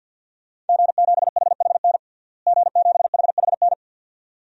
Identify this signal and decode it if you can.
Morse code, heard as a keyed tone, about 47 wpm, 700 Hz. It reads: G7HHN G7HHN
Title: DXCC-100 Call Signs - 5 Characters - Repeated 2x - 50wpm